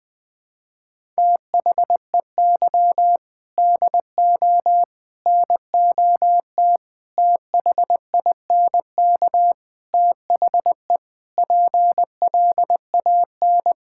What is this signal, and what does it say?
Morse code, heard as a keyed tone, about 20 wpm, 700 Hz. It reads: THEY DO NOT THINK THE PLAN
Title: Sentences with 4-Word Phrases - 20wpm